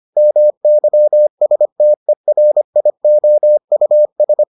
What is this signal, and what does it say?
Morse code, heard as a keyed tone, about 25 wpm, 600 Hz. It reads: MYSTERIOUS